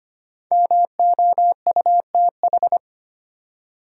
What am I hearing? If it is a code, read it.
Morse code, heard as a keyed tone, about 25 wpm, 700 Hz. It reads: MOUTH
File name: 5 Letter Words - First 3 Letters Given - 25wpm